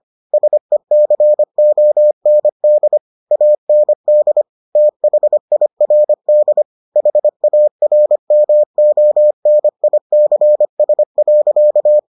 Morse code, heard as a keyed tone, about 25 wpm, 600 Hz. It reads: SECOND AND THIRD HARMONICS.